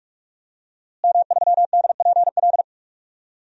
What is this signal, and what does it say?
Morse code, heard as a keyed tone, about 46 wpm, 700 Hz. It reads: M3DPL
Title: DXCC-100 Call Signs - 5 Characters - Repeated 2x - 45wpm